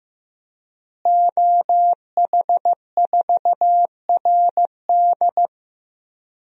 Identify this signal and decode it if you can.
Morse code, heard as a keyed tone, about 15 wpm, 700 Hz. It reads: OH4RD
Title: DXCC-100 Call Signs - 5 Characters - Repeated 3x - 15wpm